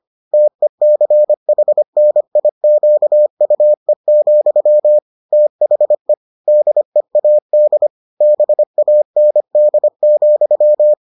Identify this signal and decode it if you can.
Morse code, heard as a keyed tone, about 25 wpm, 600 Hz. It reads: TECHNIQUE, THE DEAD BAND,